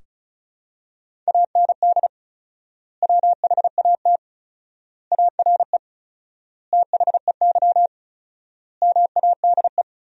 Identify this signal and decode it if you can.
Morse code, heard as a keyed tone, about 36 wpm, 700 Hz. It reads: AND WHAT ARE THEY MADE
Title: 4x Word-Spacing - Sentences from Top 200 Words - 35wpm